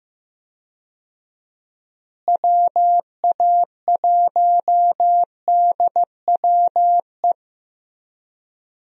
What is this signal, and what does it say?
Morse code, heard as a keyed tone, about 15 wpm, 700 Hz. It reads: WA1DWE
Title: USA Call Signs - 6 Characters - Repeated 3x - 15wpm